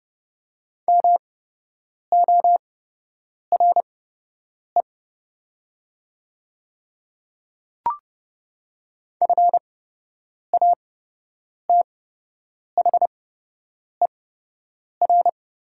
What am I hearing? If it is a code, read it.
Morse code, heard as a keyed tone, about 30 wpm, spaced as for 8 wpm, 700 Hz. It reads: MORE FATHER
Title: 8x Character Spacing - Top 300 Words - 30wpm